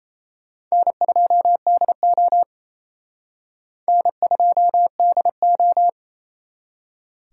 Morse code, heard as a keyed tone, about 34 wpm, 700 Hz. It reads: N2DO N2DO
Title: Speed Racing - USA 1x2 Call Signs 28wpm